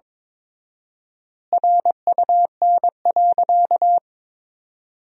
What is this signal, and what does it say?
Morse code, heard as a keyed tone, about 22 wpm, 700 Hz. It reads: RUN.
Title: 4x Word-Spacing - Sentences from Top 500 Words - 22wpm